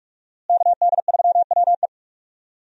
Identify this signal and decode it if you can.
Morse code, heard as a keyed tone, about 46 wpm, 700 Hz. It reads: KD3WE
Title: USA Call Signs - 5 Characters - Repeated 3x - 45wpm